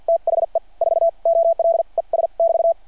Morse code, heard as a keyed tone, about 52 wpm, spaced as for 40 wpm, 600 Hz. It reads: THE 4OLES <BT>
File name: archived / CRSnet